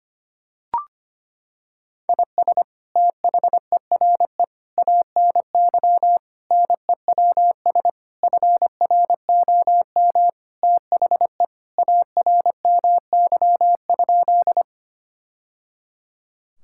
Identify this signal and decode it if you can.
Morse code, heard as a keyed tone, about 25 wpm, 700 Hz. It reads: IS THERE ANY NEWS FROM THE ARMY?
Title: Sentences from Top 1000 Words - Review Part 5 - 25wpm